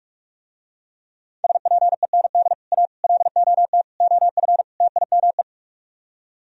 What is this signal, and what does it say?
Morse code, heard as a keyed tone, about 44 wpm, 700 Hz. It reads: SPEND A LOT OF TIME